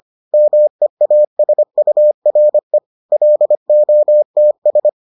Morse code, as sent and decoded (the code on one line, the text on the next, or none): -- . .- ... ..- .-. / .-.. --- - ...
MEASURE LOTS